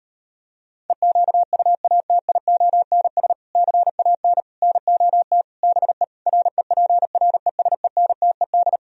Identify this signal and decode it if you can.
Morse code, heard as a keyed tone, about 38 wpm, 700 Hz. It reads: EQUATIONS CAN NOT BE REPRESENTED